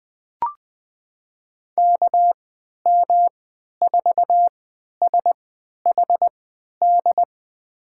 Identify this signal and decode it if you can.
Morse code, heard as a keyed tone, about 20 wpm, spaced as for 11 wpm, 700 Hz. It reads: KM4SHD
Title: USA Call Signs - 3x Character Spacing - 20wpm